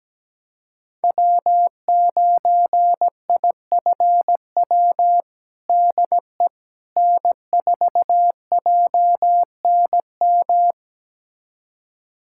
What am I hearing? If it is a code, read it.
Morse code, heard as a keyed tone, about 17 wpm, 700 Hz. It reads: W9IFW DE N4JNM
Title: USA Callsign DE Callsign - 5 Characters - 17wpm